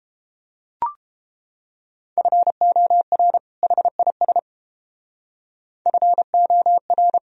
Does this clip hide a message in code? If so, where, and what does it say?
Yes: Morse code from 2.2 to 7.2 s, heard as a keyed tone, about 29 wpm, 700 Hz. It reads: FOR HIS FOR